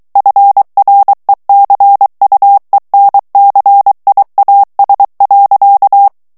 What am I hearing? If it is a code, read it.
Morse code, heard as a keyed tone, about 23 wpm, 800 Hz. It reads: FRECUENCIAS.
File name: ESPAGNOL TEXTE EN CLAIR 2 VITESSE 22 MOTS MINUTES